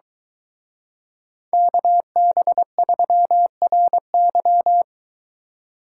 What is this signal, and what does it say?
Morse code, heard as a keyed tone, about 22 wpm, 700 Hz. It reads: KB3RY